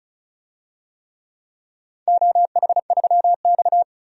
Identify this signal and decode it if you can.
Morse code, heard as a keyed tone, about 36 wpm, 700 Hz. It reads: OH3X
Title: DXCC-100 Call Signs - 4 Characters - 35wpm